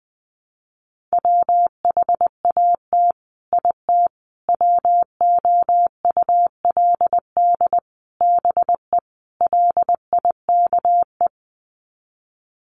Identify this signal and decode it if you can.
Morse code, heard as a keyed tone, about 20 wpm, 700 Hz. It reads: WHAT IT WOULD BE LIKE
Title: Sets of 5 Words Encore 20wpm